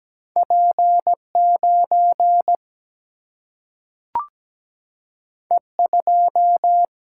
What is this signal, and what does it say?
Morse code, heard as a keyed tone, about 17 wpm, 700 Hz. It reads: P 9 E 2
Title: Single-Letter Single-Number 17wpm